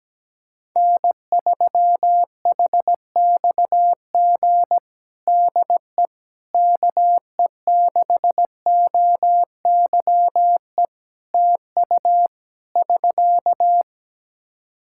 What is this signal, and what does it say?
Morse code, heard as a keyed tone, about 17 wpm, 700 Hz. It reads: N3HXG DE KE6OYE TU <SK>
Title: Practice QSO - Exchange 4 - Ending - 17wpm